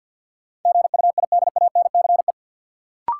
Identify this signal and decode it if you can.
Morse code, heard as a keyed tone, about 53 wpm, 700 Hz. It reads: GUIDANCE
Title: Sentences from Top 1600 Words - New Word Drill - 50wpm